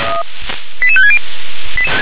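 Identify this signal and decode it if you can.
Five-tone selective call (ZVEI), heard as quick runs of five tones, tones of about 70 ms. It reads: 88560
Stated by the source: Five-tone ZVEI selective call received on an SDR.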